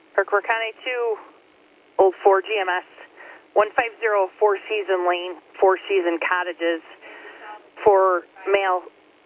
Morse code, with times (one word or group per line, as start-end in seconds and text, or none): none